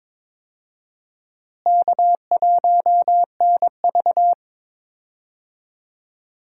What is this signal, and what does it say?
Morse code, heard as a keyed tone, about 22 wpm, 700 Hz. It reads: K1NV
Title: USA Call Signs - 4 Characters - Repeated 3x - 22wpm